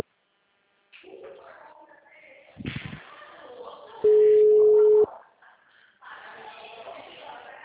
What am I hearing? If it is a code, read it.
Ringback tone (Europe, 425 Hz), with speech.
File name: out- -2023- - - .12050